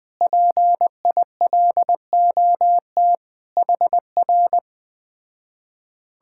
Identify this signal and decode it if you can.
Morse code, heard as a keyed tone, about 20 wpm, 700 Hz. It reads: PILOT HR
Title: Speed-Racing Common QSO Phrases - 17wpm